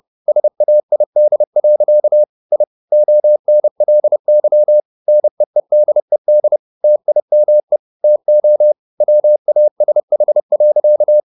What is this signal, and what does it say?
Morse code, heard as a keyed tone, about 30 wpm, 600 Hz. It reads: SAID. I ONLY NEEDED TIME TO WASH.
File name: the old man and the sea / the old man and the sea 1 30wpm